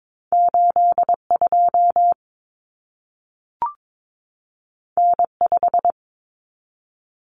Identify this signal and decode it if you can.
Morse code, heard as a keyed tone, about 22 wpm, 700 Hz. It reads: 82 N5
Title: Common Cut Numbers - 2-Digit 22wpm